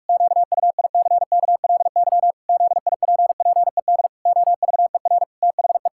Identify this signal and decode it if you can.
Morse code, heard as a keyed tone, about 48 wpm, 700 Hz. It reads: QUICKLY ZIPPED OVER THE
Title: Pangram Challenge v2 - 45wpm